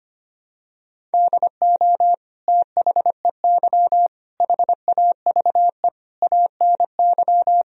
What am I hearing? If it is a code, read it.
Morse code, heard as a keyed tone, about 25 wpm, 700 Hz. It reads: DO THEY HAVE ANY
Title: Sentences from Top 1000 Words - Review Part 3 - 25wpm